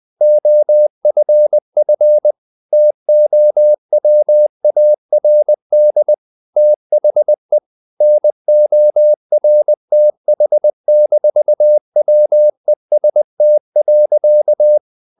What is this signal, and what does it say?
Morse code, heard as a keyed tone, about 20 wpm, 600 Hz. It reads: OFF TOWARD THE NORTH-WEST.
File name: the old man and the sea / the old man and the sea part 1 20wpm